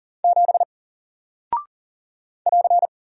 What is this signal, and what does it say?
Morse code, heard as a keyed tone, about 39 wpm, 700 Hz. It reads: 7 <AR>
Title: Single Character (Letter:Number:Punctuation:Prosign) Alt 40wpm